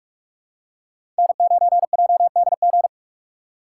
Morse code, heard as a keyed tone, about 43 wpm, 700 Hz. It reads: N9JDG
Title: USA Call Signs - 5 Characters - 6x Character Spacing - 45wpm